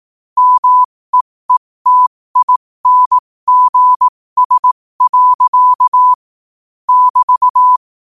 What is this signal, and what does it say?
Morse code, heard as a keyed tone, about 18 wpm, spaced as for 15 wpm, 1000 Hz. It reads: MEETINGS. <BT>
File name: qotdfiles / 15wpm